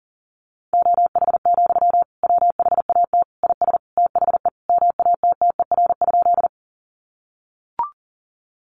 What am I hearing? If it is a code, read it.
Morse code, heard as a keyed tone, about 40 wpm, 700 Hz. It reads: OH, WHAT IS THE MATTER?
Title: Sentences from Top 700 Words 40wpm